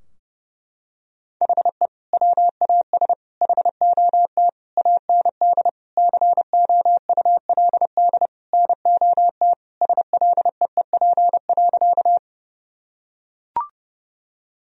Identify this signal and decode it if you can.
Morse code, heard as a keyed tone, about 30 wpm, 700 Hz. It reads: HE WAS HOT AND COULD NOT SLEEP.
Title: Sentences from Top 1000 Words - Review Part 3 - 30wpm